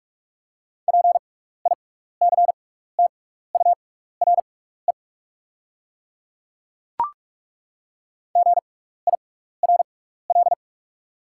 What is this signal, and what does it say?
Morse code, heard as a keyed tone, about 43 wpm, spaced as for 15 wpm, 700 Hz. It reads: PICTURE GIRL
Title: 6x Character Spacing - Top 500 Words - 45wpm